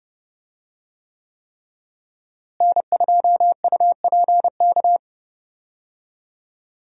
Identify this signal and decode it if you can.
Morse code, heard as a keyed tone, about 30 wpm, 700 Hz. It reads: N2UPK
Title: USA Call Signs - 5 Characters - Repeated 3x - 30wpm